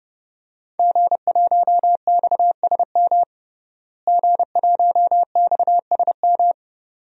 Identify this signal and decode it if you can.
Morse code, heard as a keyed tone, about 30 wpm, 700 Hz. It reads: G1XSM G1XSM
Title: DXCC-100 Call Signs - 5 Characters - Repeated 2x - 30wpm